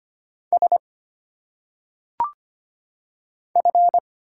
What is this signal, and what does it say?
Morse code, heard as a keyed tone, about 25 wpm, 700 Hz. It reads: S F
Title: Single Letters - Phonetic Pronunciation 25wpm